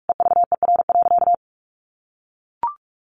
Morse code, heard as a keyed tone, about 44 wpm, 700 Hz. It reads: EVER.
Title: Sentences from Top 1400 Words 45wpm